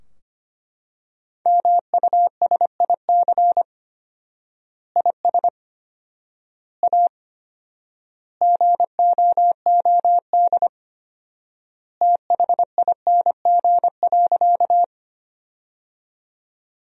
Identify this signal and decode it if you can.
Morse code, heard as a keyed tone, about 25 wpm, 700 Hz. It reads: MUSIC IS A GOOD THING.
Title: 4x Word-Spacing - Sentences from Top 300 Words - 25wpm